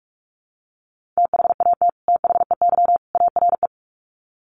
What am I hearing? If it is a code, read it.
Morse code, heard as a keyed tone, about 44 wpm, 700 Hz. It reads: THAT THEY ARE
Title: Sets of 3 Words 45wpm